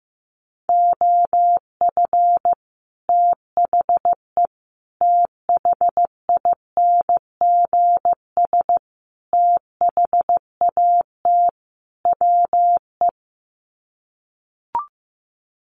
Morse code, heard as a keyed tone, about 15 wpm, 700 Hz. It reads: OF THE THINGS THAT WE
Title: Sets of 5 Words Encore 15wpm